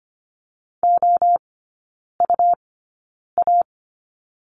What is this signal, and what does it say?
Morse code, heard as a keyed tone, about 25 wpm, spaced as for 9 wpm, 700 Hz. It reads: OUA
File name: Lesson 17e - Letter u - Three Characters 25wpm